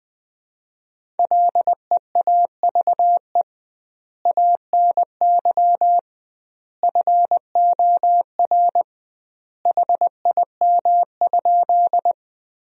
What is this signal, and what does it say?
Morse code, heard as a keyed tone, about 20 wpm, 700 Hz. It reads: LEAVE ANY FOR HIM?
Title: 2x Word-Spacing - Sentences from Top 400 Words - 20wpm